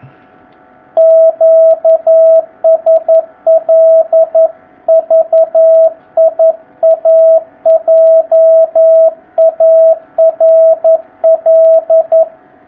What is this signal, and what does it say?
Morse code, heard as a keyed tone, about 11 wpm, 660 Hz, with radio noise. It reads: QSL VIA JARL
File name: jh8cbh / 2019cw